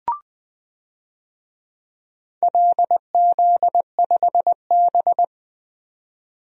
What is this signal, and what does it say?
Morse code, heard as a keyed tone, about 20 wpm, 700 Hz. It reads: LZ5B
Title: DXCC-100 Call Signs - 4 Characters - 20wpm